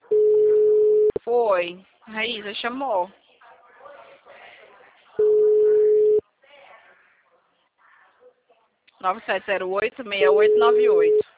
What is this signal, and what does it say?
Ringback tone (Europe, 425 Hz), with speech.
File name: out- -2024- - - .57240